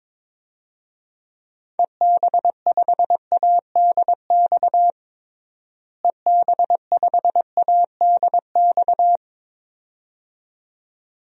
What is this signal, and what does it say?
Morse code, heard as a keyed tone, about 22 wpm, 700 Hz. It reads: EB5ADX EB5ADX
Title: DXCC-100 Call Signs - 6 Characters - Repeated 2x - 22wpm